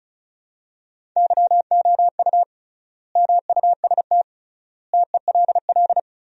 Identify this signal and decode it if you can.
Morse code, heard as a keyed tone, about 34 wpm, 700 Hz. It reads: YOU MUST TELL
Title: 3x Word-Spacing - Sentences from Top 200 Words - 35wpm